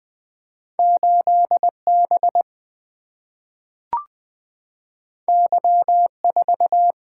Morse code, heard as a keyed tone, about 20 wpm, 700 Hz. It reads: 8B Y4
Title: Single-Letter Single-Number 20wpm